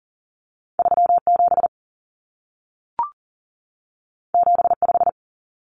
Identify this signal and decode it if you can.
Morse code, heard as a keyed tone, about 40 wpm, 700 Hz. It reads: 37 75
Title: Common Cut Numbers - 2-Digit 40wpm